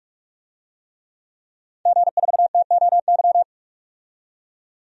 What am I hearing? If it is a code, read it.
Morse code, heard as a keyed tone, about 43 wpm, 700 Hz. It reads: G4TOY
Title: DXCC-100 Call Signs - 5 Characters - 45wpm